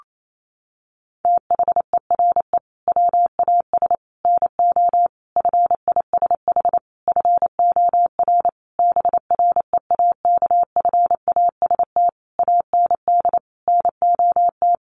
Morse code, heard as a keyed tone, about 28 wpm, 700 Hz. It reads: THERE WAS NO FISH FOR BREAKFAST AND NOT
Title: Sentences from Top 1500 Words 28wpm